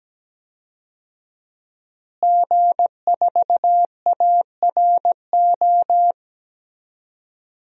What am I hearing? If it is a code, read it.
Morse code, heard as a keyed tone, about 17 wpm, 700 Hz. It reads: G4ARO
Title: DXCC-100 Call Signs - 5 Characters - Repeated 2x - 17wpm